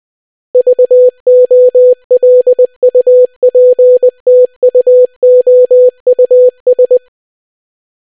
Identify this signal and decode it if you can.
Morse code, heard as a keyed tone, about 20 wpm, 500 Hz. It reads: VOLUPTUOUS